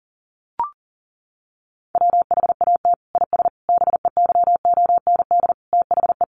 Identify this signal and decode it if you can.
Morse code, heard as a keyed tone, about 40 wpm, 700 Hz. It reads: WHAT IS BEYOND THE
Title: Sentences from Top 1100 Words 40wpm